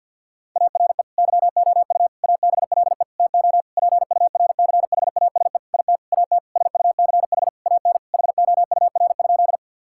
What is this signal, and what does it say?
Morse code, heard as a keyed tone, about 50 wpm, 700 Hz. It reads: ARE YOU ABLE TO PURCHASE IT AT SUCH AN HOUR?
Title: Sentences from Top 1000 Words - Review Part 5 - 50wpm